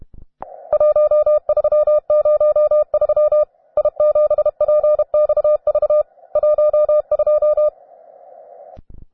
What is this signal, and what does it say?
Morse code, heard as a keyed tone, about 32 wpm, 600 Hz. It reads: 1303 I7PXV 12
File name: cw advanced